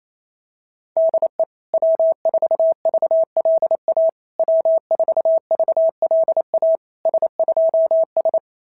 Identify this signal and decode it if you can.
Morse code, heard as a keyed tone, about 27 wpm, 660 Hz. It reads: DE W4VLA W4VLA S2S
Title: SOTA QSO Practice 28wpm